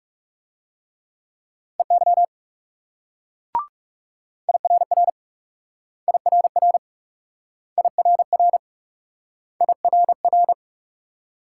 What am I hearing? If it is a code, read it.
Morse code, heard as a keyed tone, about 41 wpm, 700 Hz. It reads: EY IRR IRR IRR IRR